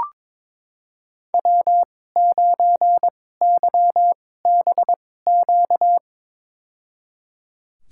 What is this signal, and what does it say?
Morse code, heard as a keyed tone, about 22 wpm, spaced as for 16 wpm, 700 Hz. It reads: W9YBQ
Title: USA Call Signs - 2x Character Spacing - 22wpm